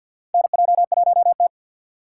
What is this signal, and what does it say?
Morse code, heard as a keyed tone, about 48 wpm, 700 Hz. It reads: NJ1T
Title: K1USN SST QSO Practice - 50wpm